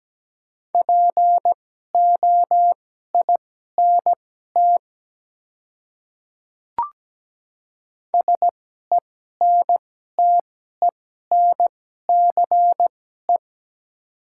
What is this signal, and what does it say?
Morse code, heard as a keyed tone, about 17 wpm, spaced as for 12 wpm, 700 Hz. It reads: POINT SENTENCE
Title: 2x Character Spacing - Top 200 Words - 17wpm